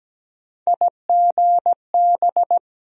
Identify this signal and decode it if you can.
Morse code, heard as a keyed tone, about 17 wpm, 700 Hz. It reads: IGB
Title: USA Call Sign Postfixes - 3 Letters 17wpm